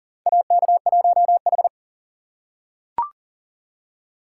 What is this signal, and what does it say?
Morse code, heard as a keyed tone, about 40 wpm, 700 Hz. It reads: AK1H
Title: USA Call Signs - 4 Characters - 3x Character Spacing - 40wpm